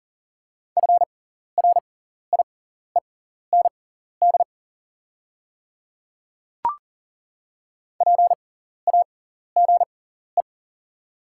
Morse code, heard as a keyed tone, about 37 wpm, spaced as for 14 wpm, 700 Hz. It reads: FRIEND PAGE
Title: 6x Character Spacing - Top 500 Words - 40wpm